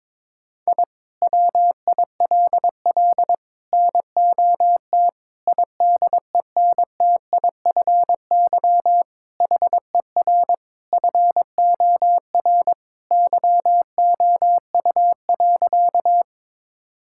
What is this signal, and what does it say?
Morse code, heard as a keyed tone, about 22 wpm, 700 Hz. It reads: I WILL NOT IDENTIFY HER FOR YOU.
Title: Sentences from Top 1000 Words - Review Part 6 - 22wpm